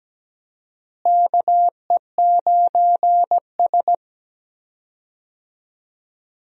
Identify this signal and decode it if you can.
Morse code, heard as a keyed tone, about 17 wpm, 700 Hz. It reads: KE9S
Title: USA Call Signs - 4 Characters - Repeated 2x - 17wpm